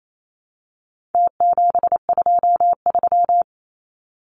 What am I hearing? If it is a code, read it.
Morse code, heard as a keyed tone, about 28 wpm, 700 Hz. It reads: T723
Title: Numbers - Contest Style - 4-Digit 28wpm